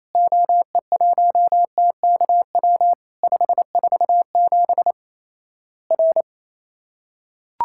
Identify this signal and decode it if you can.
Morse code, heard as a keyed tone, about 27 wpm, 700 Hz. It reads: OE1TKW 547 R
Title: Worked All Europe DX Contest - QTC Exchange - 28wpm